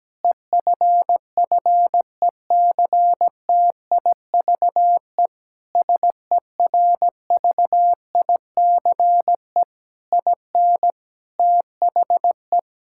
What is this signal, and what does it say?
Morse code, heard as a keyed tone, about 17 wpm, 700 Hz. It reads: EFFECTIVE SERVICE IN THE
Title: Sentences from Top 1000 Words - Review Part 6 - 17wpm